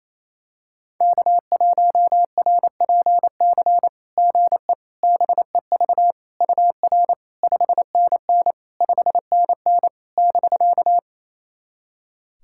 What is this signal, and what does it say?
Morse code, heard as a keyed tone, about 27 wpm, 690 Hz. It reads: K1RPC GE BEV UR 5NN 5NN <BK>
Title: SOTA QSO Practice 28wpm